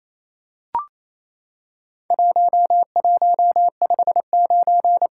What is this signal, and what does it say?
Morse code, heard as a keyed tone, about 27 wpm, 700 Hz. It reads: 1159
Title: Worked All Europe DX Contest - QTC Exchange - 28wpm